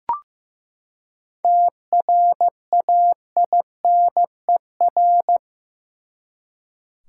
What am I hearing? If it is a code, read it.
Morse code, heard as a keyed tone, about 15 wpm, 700 Hz. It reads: TRAINER
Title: Seven Letter Words - 15wpm